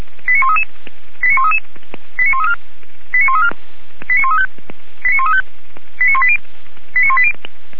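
Five-tone selective call (ZVEI), heard as quick runs of five tones, tones of about 70 ms. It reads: 89133 89133 89134 89134 89135 89135 89180
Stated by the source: ZVEI five-tone selective call 89131, received by SDR.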